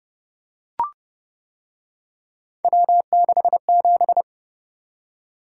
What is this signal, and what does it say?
Morse code, heard as a keyed tone, about 29 wpm, 700 Hz. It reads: W67